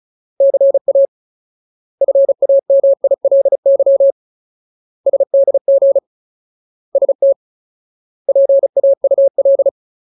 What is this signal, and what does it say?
Morse code, heard as a keyed tone, about 34 wpm, 550 Hz. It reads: CA FAMILY SDG ST PAUL
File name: Ultimate Word List / 35WPM Ultimate